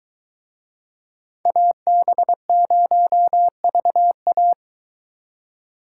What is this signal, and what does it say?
Morse code, heard as a keyed tone, about 23 wpm, 700 Hz. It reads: AB0VA